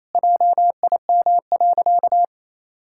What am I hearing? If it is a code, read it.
Morse code, heard as a keyed tone, about 28 wpm, 700 Hz. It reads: JIM.
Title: Pangram Challenge v2 - 28wpm